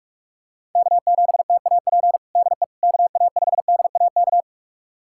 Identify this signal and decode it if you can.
Morse code, heard as a keyed tone, about 42 wpm, 700 Hz. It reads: K7TAP DE KA5DAK
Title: USA Callsign DE Callsign 45wpm